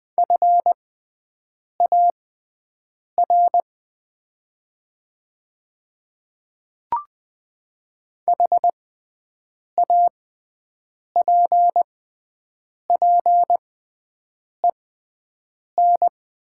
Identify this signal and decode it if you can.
Morse code, heard as a keyed tone, about 20 wpm, spaced as for 7 wpm, 700 Hz. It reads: FAR HAPPEN
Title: 6x Character Spacing - Top 500 Words - 20wpm